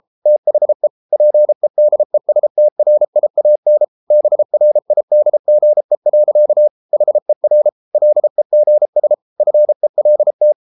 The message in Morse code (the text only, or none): THE PEDESTRIAN BRIDGE. HER LEGS FELT